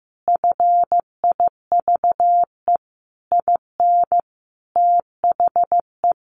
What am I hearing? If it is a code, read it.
Morse code, heard as a keyed tone, about 15 wpm, 700 Hz. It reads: FIVE IN THE
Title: Sentences from Top 1500 Words 15wpm